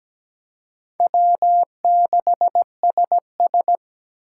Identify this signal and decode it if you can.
Morse code, heard as a keyed tone, about 20 wpm, 700 Hz. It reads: W6SS
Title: Speed Racing - USA 1x2 Call Signs 17wpm